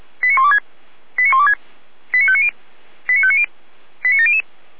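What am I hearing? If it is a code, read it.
Five-tone selective call (ZVEI), heard as quick runs of five tones, tones of about 70 ms. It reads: 89126 89126 89590 89590 89700
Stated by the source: ZVEI five-tone call 89115 received by SDR.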